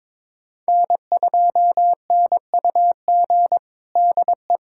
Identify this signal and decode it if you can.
Morse code, heard as a keyed tone, about 22 wpm, 700 Hz. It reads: N2NUG DE
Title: USA Callsign DE Callsign - 5 Characters - 22wpm